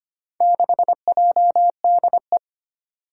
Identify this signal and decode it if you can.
Morse code, heard as a keyed tone, about 25 wpm, 700 Hz. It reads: 6JDE
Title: Four Challenging Letter-Number Combinations 25wpm